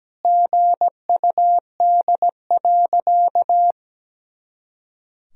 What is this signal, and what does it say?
Morse code, heard as a keyed tone, about 17 wpm, 700 Hz. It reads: GUD.
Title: Sentences from Top 200 Words and Abbreviations 17wpm